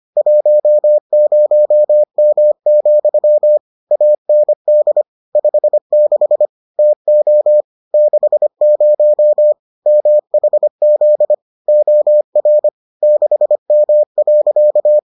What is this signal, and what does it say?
Morse code, heard as a keyed tone, about 25 wpm, 600 Hz. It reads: 10M, AND 56 TO 60 MHZ OR 6M.